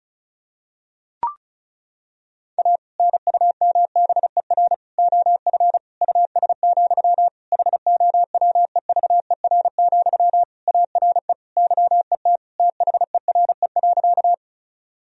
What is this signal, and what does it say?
Morse code, heard as a keyed tone, about 36 wpm, 700 Hz. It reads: A NUMBER OF US, HOWEVER, ARE YET THERE.
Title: Sentences from Top 1000 Words - Review Part 1 - 35wpm